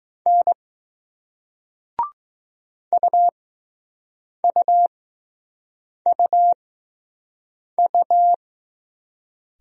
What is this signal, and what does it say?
Morse code, heard as a keyed tone, about 20 wpm, spaced as for 6 wpm, 700 Hz. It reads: N UUUU